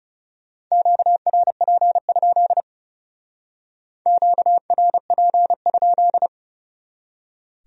Morse code, heard as a keyed tone, about 38 wpm, 700 Hz. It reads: QRP? QRP?